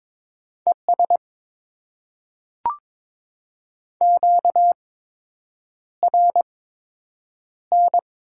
Morse code, heard as a keyed tone, about 21 wpm, spaced as for 8 wpm, 700 Hz. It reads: ES QRN
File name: Top 90 QSO Elements - 8x Character Spacing - 22wpm